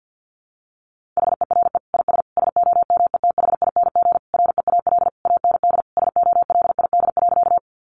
Morse code, heard as a keyed tone, about 50 wpm, 700 Hz. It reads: HERE IS SOMETHING REAL AND SOLID.
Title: Sentences from Top 1500 Words 50wpm